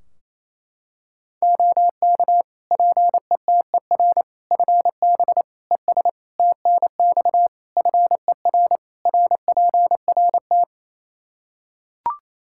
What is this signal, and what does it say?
Morse code, heard as a keyed tone, about 28 wpm, 700 Hz. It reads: OK PETER FB ES TNX FER RPRT
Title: K1USN SST QSO Practice 28wpm